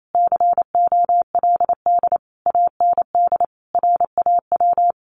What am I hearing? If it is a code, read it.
Morse code, heard as a keyed tone, about 28 wpm, 700 Hz. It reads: COLD AND RAW
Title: Sentences from Top 1500 Words 28wpm